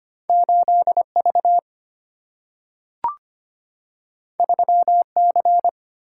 Morse code, heard as a keyed tone, about 25 wpm, 700 Hz. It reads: 8V 3C
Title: Single-Letter Single-Number 25wpm